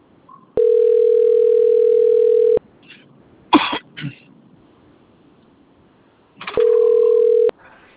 Ringback tone (North America), 440 + 480 Hz.